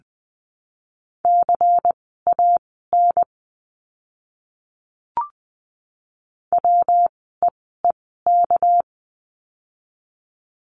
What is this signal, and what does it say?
Morse code, heard as a keyed tone, about 20 wpm, spaced as for 14 wpm, 700 Hz. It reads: CAN WEEK
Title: 244 Lesson 39f - Letter z - Words 20wpm